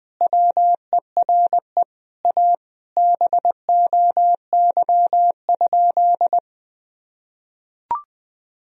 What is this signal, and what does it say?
Morse code, heard as a keyed tone, about 21 wpm, 700 Hz. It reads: WERE A BOY?
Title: Sentences from Top 1000 Words - Review Part 1 - 20wpm